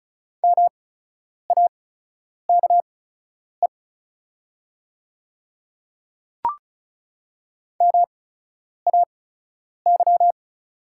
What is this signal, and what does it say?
Morse code, heard as a keyed tone, about 36 wpm, spaced as for 10 wpm, 700 Hz. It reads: MAKE MAY
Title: 8x Character Spacing - Top 100 Words - 35wpm